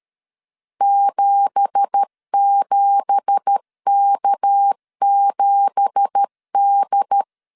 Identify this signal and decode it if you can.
Morse code, heard as a keyed tone, about 13 wpm, 800 Hz. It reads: 77K7D
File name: LEZIONI / 9 2867 12wpm 2322